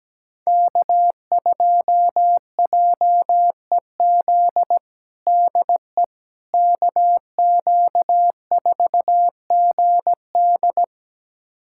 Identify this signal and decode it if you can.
Morse code, heard as a keyed tone, about 17 wpm, 700 Hz. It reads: K2JEZ DE KQ4GD
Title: USA Callsign DE Callsign - 5 Characters - 17wpm